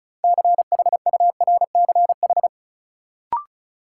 Morse code, heard as a keyed tone, about 36 wpm, 700 Hz. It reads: CHURCH
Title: Sets of 5 Words - New Word Drill - 35wpm